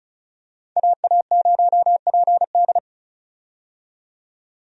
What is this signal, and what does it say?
Morse code, heard as a keyed tone, about 36 wpm, 700 Hz. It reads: AA0PD
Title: USA Call Signs - 5 Characters - 35wpm